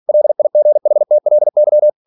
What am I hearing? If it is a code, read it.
Morse code, heard as a keyed tone, about 46 wpm, 600 Hz. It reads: LIGHTLY